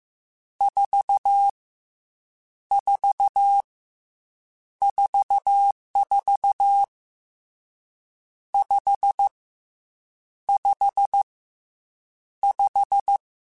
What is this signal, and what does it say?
Morse code, heard as a keyed tone, about 15 wpm, 800 Hz. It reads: 4 4 44 5 5 5